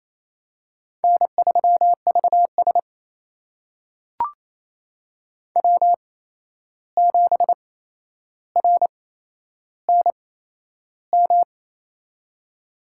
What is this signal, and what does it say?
Morse code, heard as a keyed tone, about 28 wpm, spaced as for 11 wpm, 700 Hz. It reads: N3VS W7RNM
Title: USA Call Signs - 8x Character Spacing - 28wpm